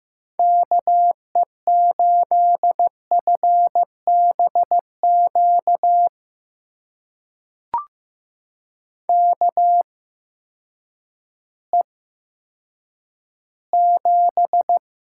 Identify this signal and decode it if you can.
Morse code, heard as a keyed tone, about 15 wpm, spaced as for 9 wpm, 700 Hz. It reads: KE8FBQ KE7
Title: USA Call Signs - 6 Characters - 8x Character Spacing - 15wpm